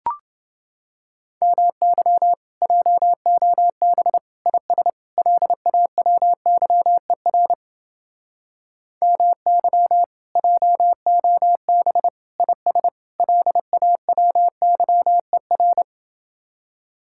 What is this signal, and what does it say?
Morse code, heard as a keyed tone, about 28 wpm, 700 Hz. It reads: MY JOB IS LAWYER MY JOB IS LAWYER